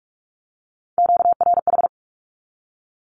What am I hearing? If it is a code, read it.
Morse code, heard as a keyed tone, about 46 wpm, 700 Hz. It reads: QRH
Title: Q-Codes with Spoken Meaning - 45wpm